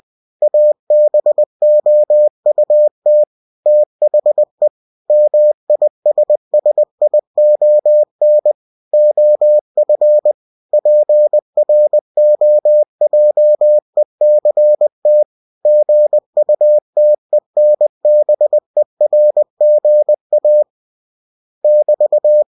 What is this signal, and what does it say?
Morse code, heard as a keyed tone, about 20 wpm, 600 Hz. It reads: ABOUT THE MISSION OF PROJECT GUTENBERGA <BT>
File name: cw-books / winnie-the-pooh / winnie-20wpm-page-0120